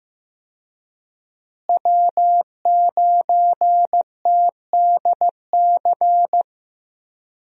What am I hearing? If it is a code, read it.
Morse code, heard as a keyed tone, about 15 wpm, 700 Hz. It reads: W9TDC
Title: USA Call Signs - 5 Characters - Repeated 2x - 15wpm